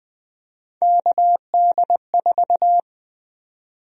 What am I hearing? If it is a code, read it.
Morse code, heard as a keyed tone, about 20 wpm, 700 Hz. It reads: KD4
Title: USA Call Sign Prefixes - 20wpm